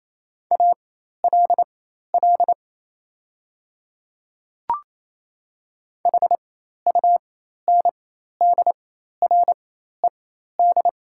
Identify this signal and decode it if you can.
Morse code, heard as a keyed tone, about 27 wpm, spaced as for 13 wpm, 700 Hz. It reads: ALL HUNDRED
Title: 4x Character Spacing - Top 500 Words - 28wpm